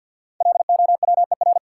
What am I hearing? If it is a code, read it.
Morse code, heard as a keyed tone, about 51 wpm, 700 Hz. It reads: LOWER
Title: Sentences from Top 700 Words - New Word Drill - 50wpm